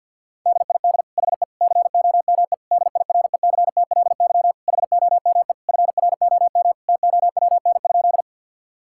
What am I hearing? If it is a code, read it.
Morse code, heard as a keyed tone, about 50 wpm, 700 Hz. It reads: DID HE COME DIRECTLY HOME FROM TOWN?
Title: Sentences from Top 1000 Words - Review Part 5 - 50wpm